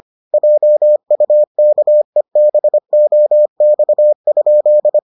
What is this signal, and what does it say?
Morse code, heard as a keyed tone, about 25 wpm, 600 Hz. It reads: JUKEBOX?